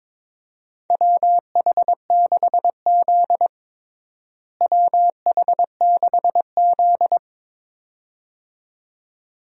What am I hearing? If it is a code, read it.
Morse code, heard as a keyed tone, about 22 wpm, 700 Hz. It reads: WH6Z WH6Z
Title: USA Call Signs - 4 Characters - Repeated 2x - 22wpm